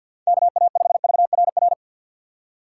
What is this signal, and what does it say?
Morse code, heard as a keyed tone, about 47 wpm, 700 Hz. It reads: KA5VRR
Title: USA Call Signs - 6 Characters - 4x Character Spacing - 50wpm